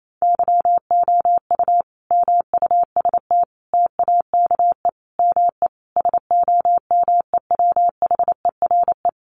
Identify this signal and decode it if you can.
Morse code, heard as a keyed tone, about 28 wpm, 700 Hz. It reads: YOU MUST TAKE ME SOMEWHERE
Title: Sentences from Top 1400 Words 28wpm